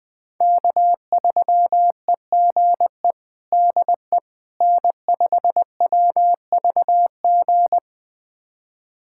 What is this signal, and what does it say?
Morse code, heard as a keyed tone, about 21 wpm, 700 Hz. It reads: K3EGE DE N5WVG
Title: USA Callsign DE Callsign - 5 Characters - 20wpm